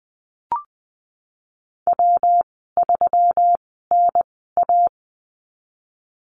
Morse code, heard as a keyed tone, about 20 wpm, spaced as for 15 wpm, 700 Hz. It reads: W3NA
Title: 140 Lesson 23g - Letter f - Call Signs 20wpm